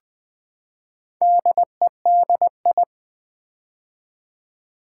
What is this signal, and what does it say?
Morse code, heard as a keyed tone, about 20 wpm, 700 Hz. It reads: DEDI
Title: Build A Word - 4 Letters At A Time - 10-Letter Words - 20wpm